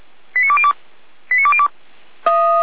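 Five-tone selective call (ZVEI), heard as quick runs of five tones, tones of about 70 ms. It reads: 89202 89202
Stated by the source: ZVEI five-tone selective call 89230 received by SDR, from the Feuerwehr Orlamünde audio server.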